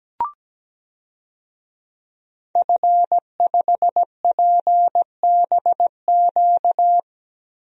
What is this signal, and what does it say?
Morse code, heard as a keyed tone, about 17 wpm, 700 Hz. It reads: F5PBQ
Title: DXCC-100 Call Signs - 5 Characters - 17wpm